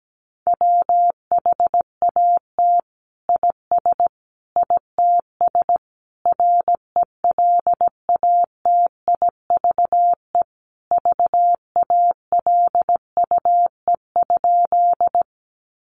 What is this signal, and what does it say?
Morse code, heard as a keyed tone, about 17 wpm, 700 Hz. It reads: WHAT IS ITS RELATIVE VALUE?
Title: Sentences from Top 1400 Words 17wpm